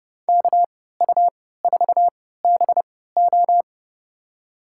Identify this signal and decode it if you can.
Morse code, heard as a keyed tone, about 32 wpm, spaced as for 17 wpm, 700 Hz. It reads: KU4BO
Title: USA Call Signs - 5 Characters - 3x Character Spacing - 30wpm